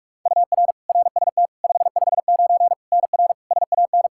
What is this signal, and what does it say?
Morse code, heard as a keyed tone, about 41 wpm, 700 Hz. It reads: UR RST 559 NR SAN
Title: Practice QSO - Exchange 1 - Beginning - 45wpm